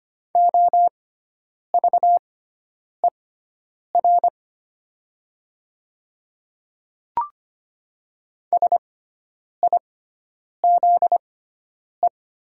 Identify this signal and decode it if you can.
Morse code, heard as a keyed tone, about 25 wpm, spaced as for 9 wpm, 700 Hz. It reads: OVER SIZE
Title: 6x Character Spacing - Top 500 Words - 25wpm